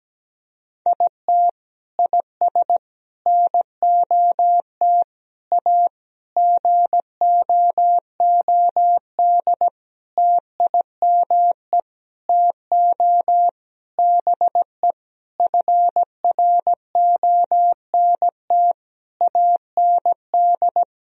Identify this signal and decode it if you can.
Morse code, heard as a keyed tone, about 17 wpm, 700 Hz. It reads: IT IS NOT A GOOD TIME TO BE FRONT AND
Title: Sentences from Top 1000 Words - Review Part 2 - 17wpm